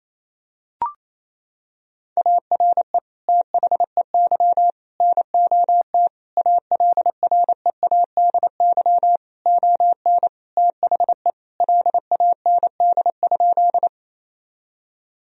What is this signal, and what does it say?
Morse code, heard as a keyed tone, about 28 wpm, 700 Hz. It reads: ARE THEY NOT ALREADY ON THE LAND?
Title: Sentences from Top 1000 Words - Review Part 1 - 28wpm